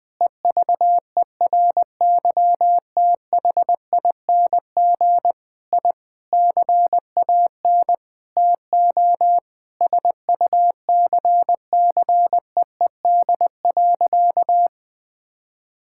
Morse code, heard as a keyed tone, about 20 wpm, 700 Hz. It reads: EVERYTHING I CAN TO SUCCEED.
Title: Sentences with 8-Word Phrases - 20wpm